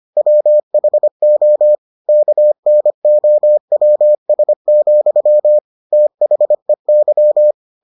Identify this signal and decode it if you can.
Morse code, heard as a keyed tone, about 25 wpm, 600 Hz. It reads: WHO KNOWS, THEY